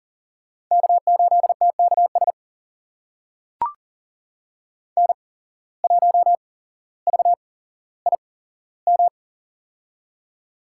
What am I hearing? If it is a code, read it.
Morse code, heard as a keyed tone, about 40 wpm, spaced as for 17 wpm, 700 Hz. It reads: K8TKS N1VIM
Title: USA Call Signs - 5 Characters - 8x Character Spacing - 40wpm